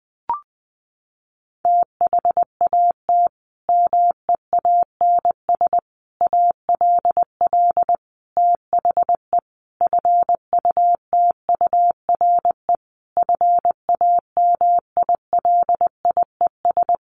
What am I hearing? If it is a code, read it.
Morse code, heard as a keyed tone, about 20 wpm, 700 Hz. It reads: THAT MEANS ALL THE FUTURE FAMILIES
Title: A Sound Of Thunder 20wpm